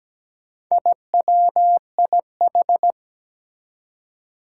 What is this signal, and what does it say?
Morse code, heard as a keyed tone, about 17 wpm, 700 Hz. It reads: IWIH
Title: Four Challenging Letter-Number Combinations 17wpm